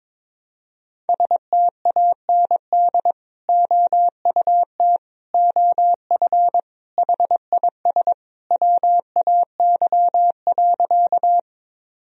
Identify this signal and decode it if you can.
Morse code, heard as a keyed tone, about 22 wpm, 700 Hz. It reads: STAND OUT OF HIS WAY.
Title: Sentences from Top 1000 Words - Review Part 1 - 22wpm